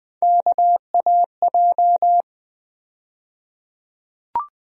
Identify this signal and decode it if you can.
Morse code, heard as a keyed tone, about 20 wpm, 700 Hz. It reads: KAJ
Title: USA Call Sign Postfixes - 3 Letters 20wpm